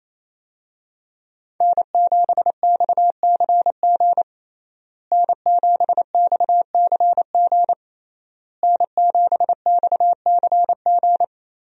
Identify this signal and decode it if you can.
Morse code, heard as a keyed tone, about 28 wpm, 700 Hz. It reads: N7XCG N7XCG N7XCG
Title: USA Call Signs - 5 Characters - Repeated 3x - 28wpm